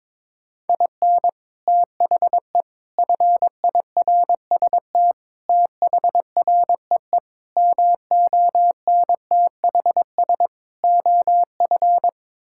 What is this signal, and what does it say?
Morse code, heard as a keyed tone, about 21 wpm, 700 Hz. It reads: IN THE FIRST THREE MONTHS OF
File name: practice / 22wpm / Sets of 8 Words - 22wpm